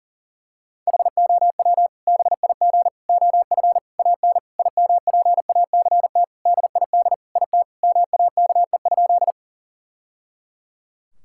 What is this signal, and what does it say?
Morse code, heard as a keyed tone, about 40 wpm, 700 Hz. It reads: HOW BIG OF AN IMPACT DID IT MAKE?
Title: Sentences from Top 1000 Words - Review Part 6 - 40wpm